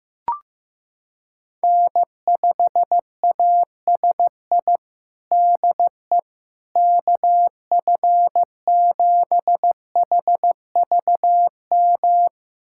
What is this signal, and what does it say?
Morse code, heard as a keyed tone, about 15 wpm, 700 Hz. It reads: N5ASI DE KF7HVM
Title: USA Callsign DE Callsign 15wpm